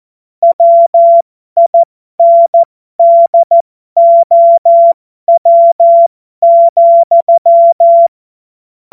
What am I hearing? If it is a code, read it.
Morse code, heard as a keyed tone, about 14 wpm, spaced as for 12 wpm, 680 Hz. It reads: WINDOW,
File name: Morsekurs / Witze